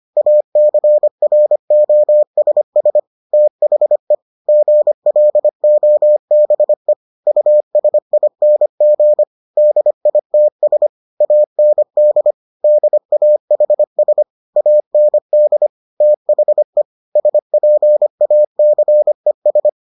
Morse code, heard as a keyed tone, about 25 wpm, 600 Hz. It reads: ACROSS THE GLOBE USING DITS AND DAHS AND THE SPACES